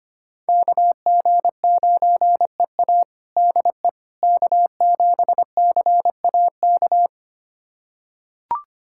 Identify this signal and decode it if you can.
Morse code, heard as a keyed tone, about 25 wpm, 700 Hz. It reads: KG9EA DE K7CAK
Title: USA Callsign DE Callsign - 5 Characters - 25wpm